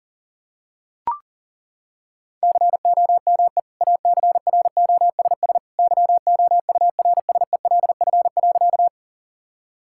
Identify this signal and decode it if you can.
Morse code, heard as a keyed tone, about 40 wpm, 700 Hz. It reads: COME ACROSS YOURSELF.
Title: Sentences from Top 1000 Words - Review Part 5 - 40wpm